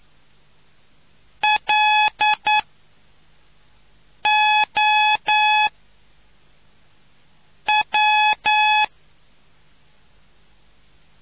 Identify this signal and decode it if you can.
Morse code, heard as a keyed tone, about 9 wpm, 850 Hz. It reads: L O W